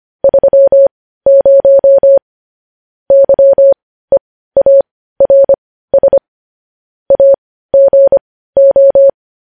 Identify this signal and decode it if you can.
Morse code, heard as a keyed tone, about 25 wpm, spaced as for 15 wpm, 550 Hz. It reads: 30 YEARS AGO
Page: news-headlines-in-morse-code-at-15-wpm